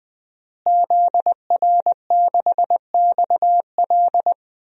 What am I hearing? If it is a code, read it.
Morse code, heard as a keyed tone, about 20 wpm, 700 Hz. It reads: ZR6XL
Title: DXCC-100 Call Signs - 5 Characters - Repeated 2x - 20wpm